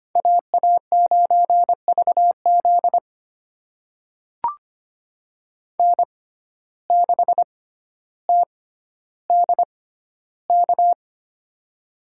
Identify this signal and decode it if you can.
Morse code, heard as a keyed tone, about 25 wpm, spaced as for 13 wpm, 700 Hz. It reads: AA9VZ N6TDK
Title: USA Call Signs - 5 Characters - 6x Character Spacing - 25wpm